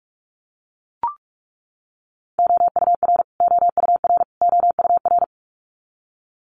Morse code, heard as a keyed tone, about 44 wpm, 700 Hz. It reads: OUR OUR OUR
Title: Top 500 Words Repeated 45wpm